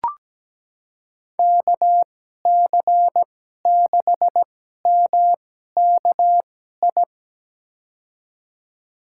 Morse code, heard as a keyed tone, about 17 wpm, spaced as for 12 wpm, 700 Hz. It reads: KC6MKI
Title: USA Call Signs - 6 Characters - 2x Character Spacing - 17wpm